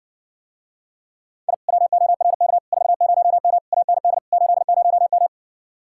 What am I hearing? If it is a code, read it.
Morse code, heard as a keyed tone, about 66 wpm, 700 Hz. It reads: I WORK 40M AND 80M